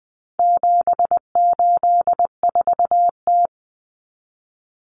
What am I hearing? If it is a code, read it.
Morse code, heard as a keyed tone, about 20 wpm, 700 Hz. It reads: 784T
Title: Common Cut Numbers - 4-Digit 20wpm